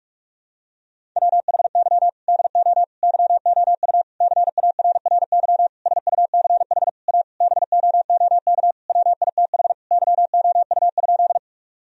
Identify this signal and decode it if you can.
Morse code, heard as a keyed tone, about 48 wpm, 700 Hz. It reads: WHY DO YOU CARRY SUCH A BOOK WITH YOU?
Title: Sentences from Top 1000 Words - Review Part 1 - 45wpm